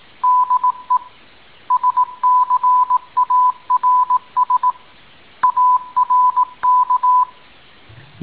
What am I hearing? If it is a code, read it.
Morse code, heard as a keyed tone, about 18 wpm, 1000 Hz. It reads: DE SCARS ARK
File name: tin-upgrade / QSO 2019 01 06 .mp3